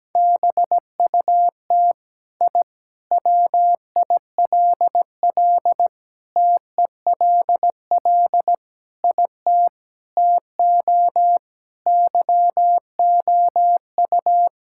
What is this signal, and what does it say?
Morse code, heard as a keyed tone, about 17 wpm, 700 Hz. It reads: BUT I WILL TELL IT TO YOU
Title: Sentences from Top 1000 Words - Review Part 5 - 17wpm